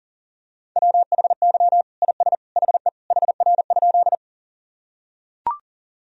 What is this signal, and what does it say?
Morse code, heard as a keyed tone, about 37 wpm, 700 Hz. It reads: WHY IS HE HR?
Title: Sentences from Top 200 Words and Abbreviations 40wpm